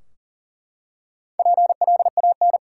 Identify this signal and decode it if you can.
Morse code, heard as a keyed tone, about 40 wpm, 700 Hz. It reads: PLAN
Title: Four Letter Words - 40wpm